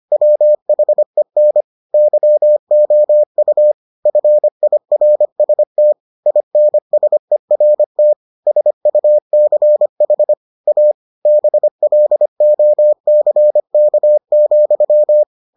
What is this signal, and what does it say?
Morse code, heard as a keyed tone, about 25 wpm, 600 Hz. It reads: WHEN YOU FIRST INSERT SUCH A BLOCK,